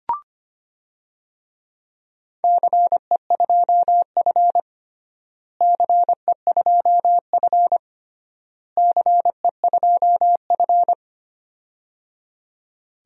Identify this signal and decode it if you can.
Morse code, heard as a keyed tone, about 25 wpm, 700 Hz. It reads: CE2F CE2F CE2F
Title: DXCC-100 Call Signs - 4 Characters - Repeated 3x - 25wpm